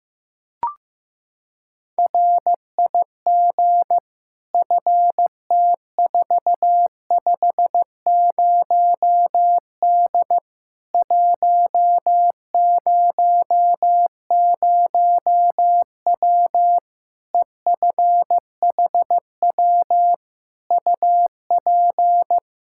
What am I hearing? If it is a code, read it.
Morse code, heard as a keyed tone, about 15 wpm, 700 Hz. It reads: RIG FT450D 100W EFHW UP
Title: K1USN SST QSO Practice 15wpm